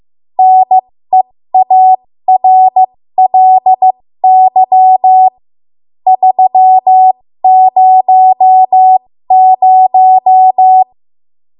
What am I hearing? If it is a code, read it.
Morse code, heard as a keyed tone, about 15 wpm, spaced as for 13 wpm, 750 Hz. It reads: NEARLY 300
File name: Morse / 13WPM